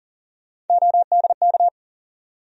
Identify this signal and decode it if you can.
Morse code, heard as a keyed tone, about 40 wpm, 700 Hz. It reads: ODK
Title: USA Call Sign Postfixes - 3 Letters 40wpm